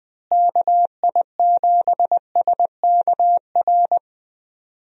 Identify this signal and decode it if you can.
Morse code, heard as a keyed tone, about 20 wpm, 700 Hz. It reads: KI7SKR
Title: USA Call Signs - 6 Characters - Repeated 2x - 20wpm